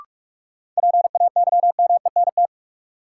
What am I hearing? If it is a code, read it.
Morse code, heard as a keyed tone, about 46 wpm, 700 Hz. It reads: PAYMENT
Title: Sentences from Top 1200 Words - New Word Drill - 45wpm